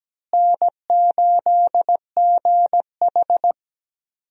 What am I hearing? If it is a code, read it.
Morse code, heard as a keyed tone, about 17 wpm, 700 Hz. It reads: N8GH
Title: USA Call Signs - 4 Characters - Repeated 3x - 17wpm